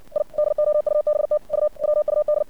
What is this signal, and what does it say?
Morse code, heard as a keyed tone, about 60 wpm, 600 Hz. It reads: I DOUBT U WUD
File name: archived / CRSnet